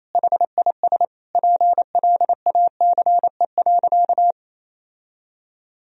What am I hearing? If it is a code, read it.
Morse code, heard as a keyed tone, about 28 wpm, 700 Hz. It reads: HIS PLACE.
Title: Sentences from Top 1000 Words - Review Part 1 - 28wpm